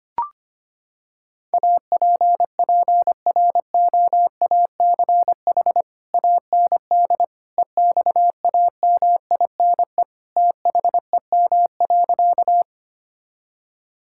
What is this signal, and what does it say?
Morse code, heard as a keyed tone, about 25 wpm, 700 Hz. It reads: APPROACH AND EXAMINE THEM.
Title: Sentences from Top 1600 Words 25wpm